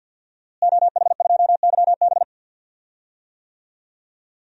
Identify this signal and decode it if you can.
Morse code, heard as a keyed tone, about 51 wpm, 700 Hz. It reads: OH2YB
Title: DXCC-100 Call Signs - 5 Characters - Repeated 2x - 50wpm